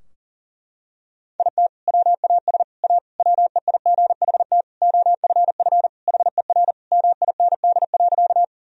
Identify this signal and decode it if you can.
Morse code, heard as a keyed tone, about 40 wpm, 700 Hz. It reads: IT WAS A WEIGHT OFF HER MIND.
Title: Sentences from Top 1000 Words - Review Part 3 - 40wpm